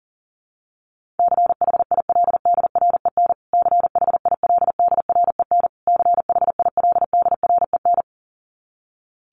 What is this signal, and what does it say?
Morse code, heard as a keyed tone, about 40 wpm, 700 Hz. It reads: CHILDREN CHILDREN CHILDREN
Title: Top 500 Words Repeated 40wpm